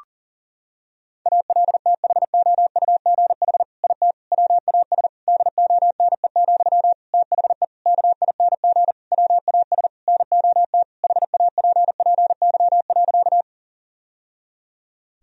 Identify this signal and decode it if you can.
Morse code, heard as a keyed tone, about 40 wpm, 700 Hz. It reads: ALTHOUGH IT WAS DONE, THE KING WAS NOT HAPPY.
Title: Sentences from Top 1000 Words - Review Part 3 - 40wpm